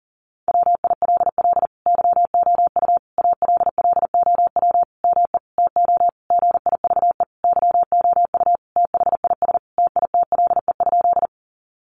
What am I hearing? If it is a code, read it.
Morse code, heard as a keyed tone, about 40 wpm, 700 Hz. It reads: WILL YOU ALLOW ME TO GIVE YOU THIS TITLE?
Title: Sentences from Top 1200 Words 40wpm